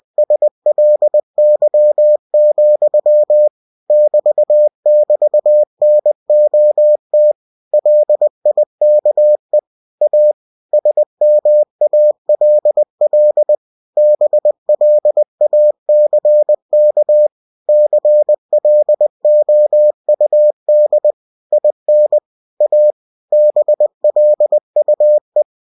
Morse code, heard as a keyed tone, about 20 wpm, 600 Hz. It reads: SLY, <BT><BT>NOT LIKE A SMALL BLACK CLOUD IN A BLUE
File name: cw-books / winnie-the-pooh / winnie-20wpm-page-0006